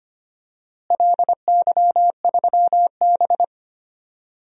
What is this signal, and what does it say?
Morse code, heard as a keyed tone, about 25 wpm, 700 Hz. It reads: LY3B
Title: DXCC-100 Call Signs - 4 Characters - Repeated 3x - 25wpm